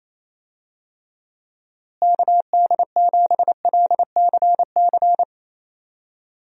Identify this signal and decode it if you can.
Morse code, heard as a keyed tone, about 28 wpm, 700 Hz. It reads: KD7LCC
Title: USA Call Signs - 6 Characters - Repeated 2x - 28wpm